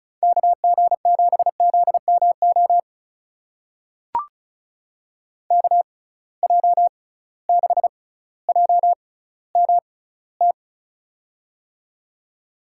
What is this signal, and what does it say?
Morse code, heard as a keyed tone, about 36 wpm, spaced as for 18 wpm, 700 Hz. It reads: KG7ZMO KJ6JMT
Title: USA Call Signs - 6 Characters - 6x Character Spacing - 35wpm